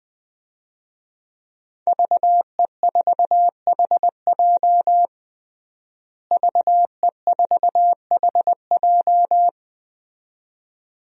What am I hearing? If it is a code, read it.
Morse code, heard as a keyed tone, about 20 wpm, 700 Hz. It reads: VE4HJ VE4HJ
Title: DXCC-100 Call Signs - 5 Characters - Repeated 2x - 20wpm